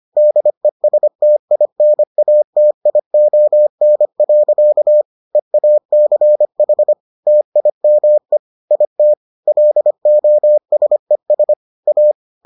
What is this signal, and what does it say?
Morse code, heard as a keyed tone, about 25 wpm, 600 Hz. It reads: DESTINATION. EACH TIME IT LOSES A